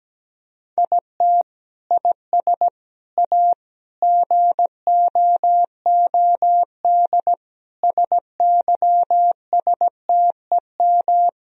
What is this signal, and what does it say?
Morse code, heard as a keyed tone, about 17 wpm, 700 Hz. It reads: IT IS A GOOD SYSTEM
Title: Sentences from Top 1000 Words - Review Part 3 - 17wpm